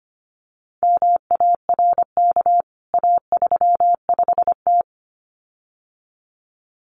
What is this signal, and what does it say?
Morse code, heard as a keyed tone, about 25 wpm, 700 Hz. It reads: MARK A35T
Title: CWT Exchange 25wpm